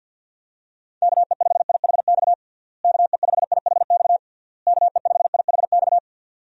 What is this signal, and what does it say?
Morse code, heard as a keyed tone, about 51 wpm, 700 Hz. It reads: KE5IHX KE5IHX KE5IHX
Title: USA Call Signs - 6 Characters - Repeated 3x - 50wpm